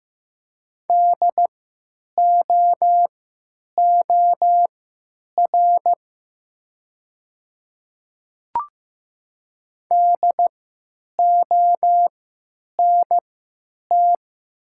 Morse code, heard as a keyed tone, about 15 wpm, spaced as for 9 wpm, 700 Hz. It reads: DOOR DONT
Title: 3x Character Spacing - Top 300 Words - 15wpm